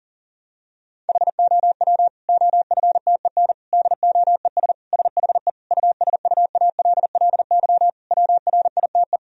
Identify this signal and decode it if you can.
Morse code, heard as a keyed tone, about 40 wpm, 700 Hz. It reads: HOW OFTEN DOES SHE USUALLY WRITE
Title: Sentences from Top 1000 Words - Review Part 3 - 40wpm